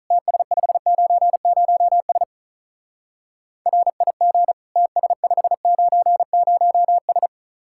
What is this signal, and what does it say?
Morse code, heard as a keyed tone, about 43 wpm, 700 Hz. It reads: TS590S RIG TS590S
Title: Speed-Racing Common QSO Phrases Encore - 35wpm